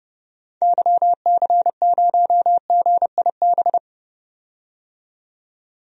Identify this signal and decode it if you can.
Morse code, heard as a keyed tone, about 30 wpm, 700 Hz. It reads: YC0GIB
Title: DXCC-100 Call Signs - 6 Characters - 30wpm